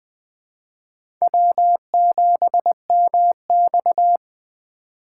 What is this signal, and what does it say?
Morse code, heard as a keyed tone, about 21 wpm, 700 Hz. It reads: W7MX
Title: USA Call Signs - 4 Characters - 3x Character Spacing - 20wpm